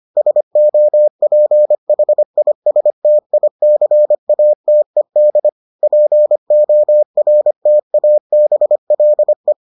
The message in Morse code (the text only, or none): SOPHISTICATED PORTABLE